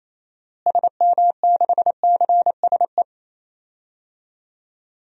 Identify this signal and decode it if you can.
Morse code, heard as a keyed tone, about 28 wpm, 700 Hz. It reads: SM6CSE
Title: DXCC-100 Call Signs - 6 Characters - Repeated 2x - 28wpm